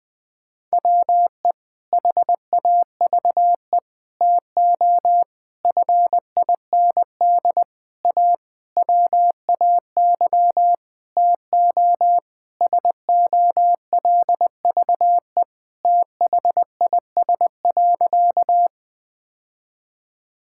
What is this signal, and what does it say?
Morse code, heard as a keyed tone, about 21 wpm, 700 Hz. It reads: WE HAVE TO FIND A WAY TO SOLVE THIS.
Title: Sentences with 6-Word Phrases - 20wpm